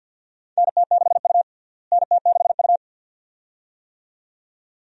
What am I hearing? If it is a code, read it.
Morse code, heard as a keyed tone, about 51 wpm, 700 Hz. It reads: NT6U NT6U
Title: USA Call Signs - 4 Characters - Repeated 2x - 50wpm